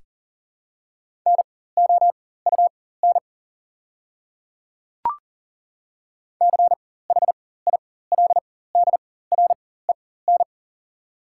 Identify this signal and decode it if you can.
Morse code, heard as a keyed tone, about 40 wpm, spaced as for 19 wpm, 700 Hz. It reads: NOUN CHILDREN
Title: 4x Character Spacing - Top 500 Words - 40wpm